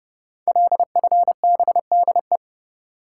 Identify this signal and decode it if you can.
Morse code, heard as a keyed tone, about 30 wpm, 700 Hz. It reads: LFBDE
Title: Code Groups - 5 Letters - 30wpm